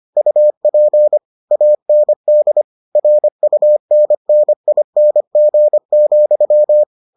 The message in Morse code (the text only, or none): UP AND RUNNING,